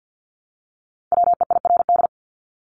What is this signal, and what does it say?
Morse code, heard as a keyed tone, about 50 wpm, 700 Hz. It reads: WEIRD
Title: Top 1251-1500 Words 50wpm